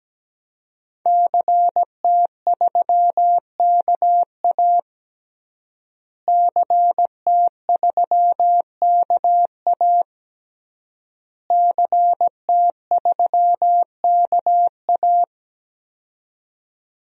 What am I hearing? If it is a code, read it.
Morse code, heard as a keyed tone, about 17 wpm, 700 Hz. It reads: CT3KA CT3KA CT3KA
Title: DXCC-100 Call Signs - 5 Characters - Repeated 3x - 17wpm